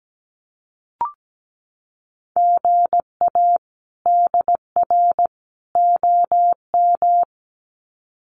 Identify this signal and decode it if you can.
Morse code, heard as a keyed tone, about 17 wpm, 700 Hz. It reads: GA DR OM
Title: Common QSO Phrases 17wpm